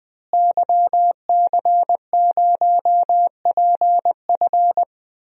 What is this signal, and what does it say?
Morse code, heard as a keyed tone, about 20 wpm, 700 Hz. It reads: YC0PF
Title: DXCC-100 Call Signs - 5 Characters - Repeated 2x - 20wpm